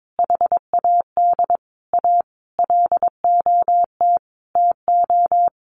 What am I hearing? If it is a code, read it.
Morse code, heard as a keyed tone, about 22 wpm, 700 Hz. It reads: HAD A LOT TO
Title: Sentences from Top 1000 Words 22wpm